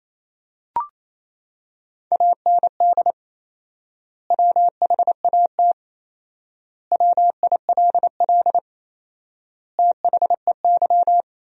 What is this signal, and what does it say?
Morse code, heard as a keyed tone, about 28 wpm, 700 Hz. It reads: AND WHAT WILL THEY
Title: 4x Word-Spacing - Sentences from Top 200 Words - 28wpm